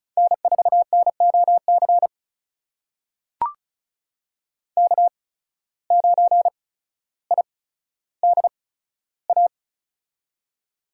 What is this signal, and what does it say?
Morse code, heard as a keyed tone, about 36 wpm, spaced as for 15 wpm, 700 Hz. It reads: N4NOC K9IDA
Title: USA Call Signs - 5 Characters - 8x Character Spacing - 35wpm